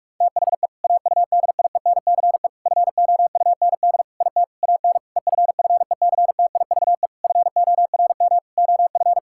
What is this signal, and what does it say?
Morse code, heard as a keyed tone, about 46 wpm, 700 Hz. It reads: THE AUDIENCE FOUND IT AN EFFECTIVE FORM OF
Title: Sentences from Top 1000 Words - Review Part 6 - 45wpm